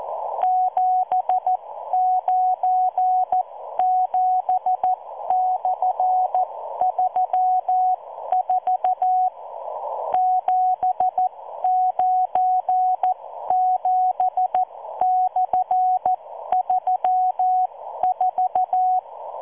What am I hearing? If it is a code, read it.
Morse code, heard as a keyed tone, about 14 wpm, spaced as for 12 wpm, 740 Hz. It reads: 797/34 797/34